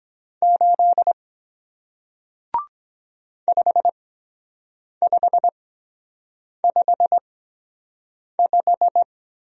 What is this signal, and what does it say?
Morse code, heard as a keyed tone, about 23 wpm, spaced as for 7 wpm, 700 Hz. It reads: 8 5555